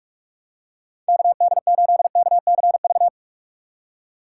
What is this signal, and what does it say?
Morse code, heard as a keyed tone, about 46 wpm, 700 Hz. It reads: KD8KCV
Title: USA Call Signs - 2x Character Spacing - 45wpm